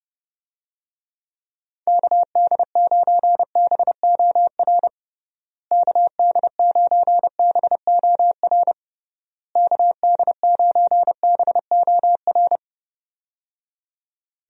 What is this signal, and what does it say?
Morse code, heard as a keyed tone, about 30 wpm, 700 Hz. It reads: KD9BOR KD9BOR KD9BOR
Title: USA Call Signs - 6 Characters - Repeated 3x - 30wpm